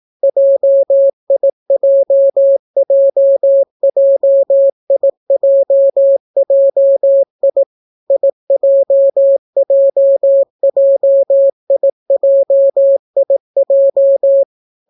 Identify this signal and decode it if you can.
Morse code, heard as a keyed tone, about 18 wpm, 550 Hz. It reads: JIJJJIJJI IJJJIJIJ